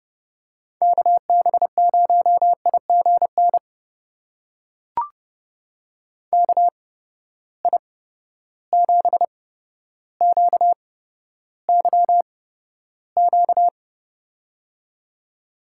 Morse code, heard as a keyed tone, about 30 wpm, spaced as for 13 wpm, 700 Hz. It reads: KB0IGN KI7QYQ
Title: USA Call Signs - 6 Characters - 8x Character Spacing - 30wpm